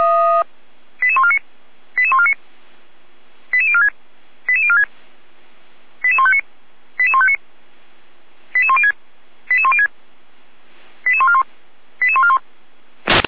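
Five-tone selective call (ZVEI), heard as quick runs of five tones, tones of about 70 ms. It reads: 88149 88149 88046 88046 88159 88159 88106 88106 88141 88141
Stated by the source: ZVEI five-tone selective call 88150, received with an SDR.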